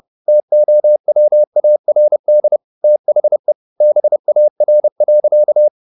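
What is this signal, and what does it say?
Morse code, heard as a keyed tone, about 30 wpm, 600 Hz. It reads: TOWARD THE BAR.
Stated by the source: Morse code at 30 wpm, the novel Never by Ken Follett in English, from CW med Gnister.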